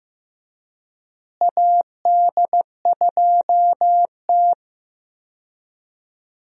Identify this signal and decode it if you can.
Morse code, heard as a keyed tone, about 15 wpm, 700 Hz. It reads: AD2T
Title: USA Call Signs - 4 Characters - Repeated 3x - 15wpm